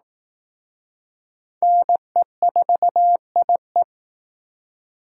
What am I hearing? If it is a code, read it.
Morse code, heard as a keyed tone, about 18 wpm, 700 Hz. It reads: NE4IE
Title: Speed Racing - USA Call Signs 15wpm